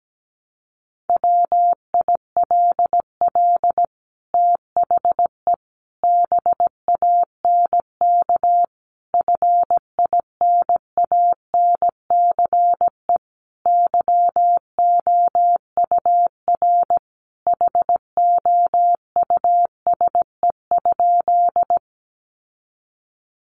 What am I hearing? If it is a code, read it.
Morse code, heard as a keyed tone, about 17 wpm, 700 Hz. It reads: WILL THE BANK FINANCE YOUR HOUSE?
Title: Sentences from Top 1500 Words 17wpm